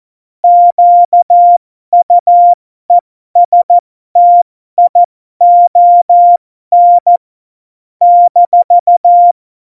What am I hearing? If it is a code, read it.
Morse code, heard as a keyed tone, about 14 wpm, spaced as for 12 wpm, 700 Hz. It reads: QUESTION -
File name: Morsekurs / Witze